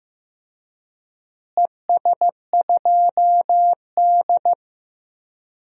Morse code, heard as a keyed tone, about 15 wpm, 700 Hz. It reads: ES2D
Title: DXCC-100 Call Signs - 4 Characters - Repeated 3x - 15wpm